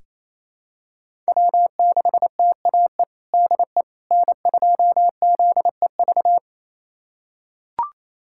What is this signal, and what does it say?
Morse code, heard as a keyed tone, about 28 wpm, 700 Hz. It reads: W6TAE DE N2ZEV
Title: USA Callsign DE Callsign - 5 Characters - 28wpm